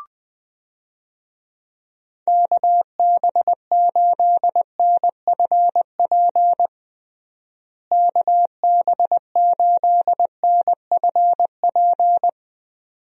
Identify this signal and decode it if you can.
Morse code, heard as a keyed tone, about 19 wpm, 700 Hz. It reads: KB8NFP KB8NFP
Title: USA Call Signs - 6 Characters - Repeated 2x - 20wpm